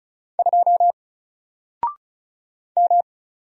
Morse code, heard as a keyed tone, about 33 wpm, 700 Hz. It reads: 2 M
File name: Single-Letter-Number-35wpm